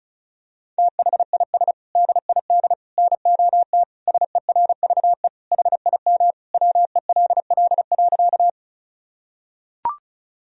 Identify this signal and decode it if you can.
Morse code, heard as a keyed tone, about 36 wpm, 700 Hz. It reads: THIS DID NOT SERVE HIM WELL.
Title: Sentences from Top 1000 Words - Review Part 2 - 35wpm